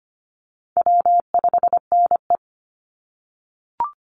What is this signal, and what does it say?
Morse code, heard as a keyed tone, about 25 wpm, 700 Hz. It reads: W5NE
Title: Easy Call Signs - 25wpm